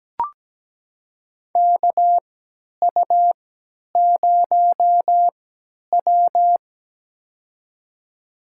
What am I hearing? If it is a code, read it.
Morse code, heard as a keyed tone, about 17 wpm, spaced as for 10 wpm, 700 Hz. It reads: KU0W
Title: USA Call Signs - 4 Characters - 3x Character Spacing - 17wpm